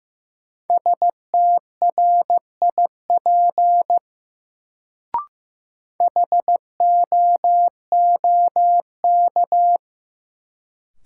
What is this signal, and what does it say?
Morse code, heard as a keyed tone, about 15 wpm, 700 Hz. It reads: STRIP HOOK
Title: Sentences from Top 1600 Words - New Word Drill Alt - 15wpm